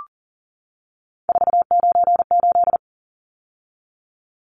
Morse code, heard as a keyed tone, about 40 wpm, 700 Hz. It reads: 498
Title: Numbers - 3-Digit 40wpm